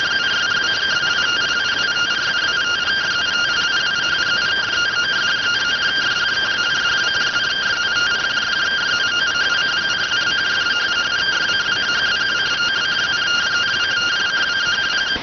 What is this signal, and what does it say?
Two-tone teleprinter signal, about 170 Hz shift.